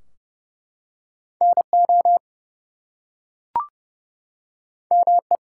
Morse code, heard as a keyed tone, about 31 wpm, 700 Hz. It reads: NO ME
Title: Two Letter Words - 30wpm